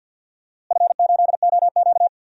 Morse code, heard as a keyed tone, about 50 wpm, 700 Hz. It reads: F8OQ
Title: DXCC-100 Call Signs - 4 Characters - Repeated 3x - 50wpm